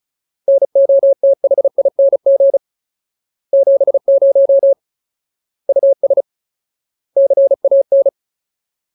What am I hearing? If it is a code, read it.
Morse code, heard as a keyed tone, about 34 wpm, 550 Hz. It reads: NOTHING 70 US CAN
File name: Ultimate Word List / 35WPM Ultimate